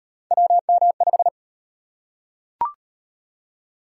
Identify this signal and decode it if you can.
Morse code, heard as a keyed tone, about 38 wpm, 700 Hz. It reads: WM5